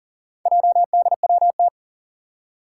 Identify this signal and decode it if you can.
Morse code, heard as a keyed tone, about 40 wpm, 700 Hz. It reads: JDWT
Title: Four Challenging Letter-Number Combinations 40wpm